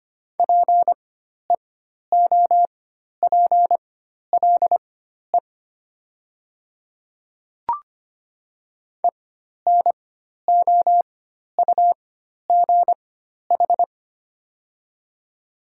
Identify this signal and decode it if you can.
Morse code, heard as a keyed tone, about 25 wpm, spaced as for 12 wpm, 700 Hz. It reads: PEOPLE ENOUGH
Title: 4x Character Spacing - Top 500 Words - 25wpm